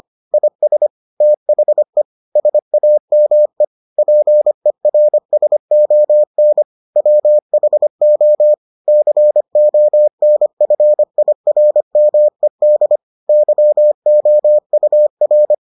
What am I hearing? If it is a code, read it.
Morse code, heard as a keyed tone, about 25 wpm, 600 Hz. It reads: IS THE SAME PERSON WHO CONFIRMED YOUR